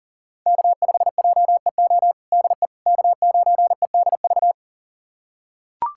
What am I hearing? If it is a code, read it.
Morse code, heard as a keyed tone, about 40 wpm, 700 Hz. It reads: K5JEO DE K9EDV
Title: USA Callsign DE Callsign - 5 Characters - 40wpm